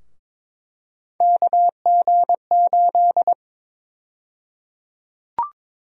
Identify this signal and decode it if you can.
Morse code, heard as a keyed tone, about 22 wpm, 700 Hz. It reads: KG8
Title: USA Call Sign Prefixes - 22wpm